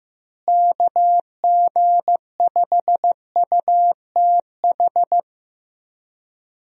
Morse code, heard as a keyed tone, about 15 wpm, 700 Hz. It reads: KG5UTH
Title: USA Call Signs - 6 Characters - Repeated 3x - 15wpm